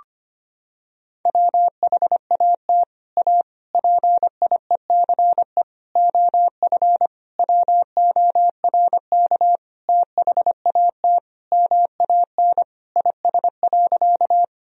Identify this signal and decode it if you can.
Morse code, heard as a keyed tone, about 25 wpm, 700 Hz. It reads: WHAT A PIECE OF WORK THAT MAN IS.
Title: Sentences from Top 1000 Words - Review Part 2 - 25wpm